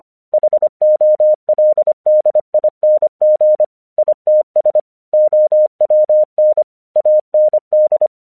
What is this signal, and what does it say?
Morse code, heard as a keyed tone, about 24 wpm, 610 Hz. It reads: HOLDING ITS OWN AND